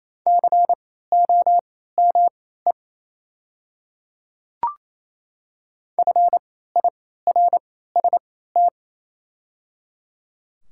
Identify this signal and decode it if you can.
Morse code, heard as a keyed tone, about 28 wpm, spaced as for 16 wpm, 700 Hz. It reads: COME FIRST
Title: 3x Character Spacing - Top 200 Words - 28wpm